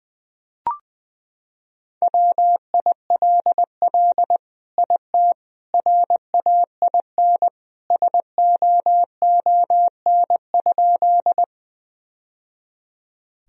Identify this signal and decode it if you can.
Morse code, heard as a keyed tone, about 20 wpm, 700 Hz. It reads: WILL IT RAIN SOON?
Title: Sentences from Top 1000 Words - Review Part 1 - 20wpm